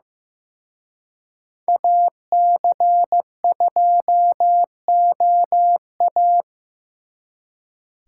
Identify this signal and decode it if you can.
Morse code, heard as a keyed tone, about 18 wpm, 700 Hz. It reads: AC2OA
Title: Speed Racing - USA Call Signs 15wpm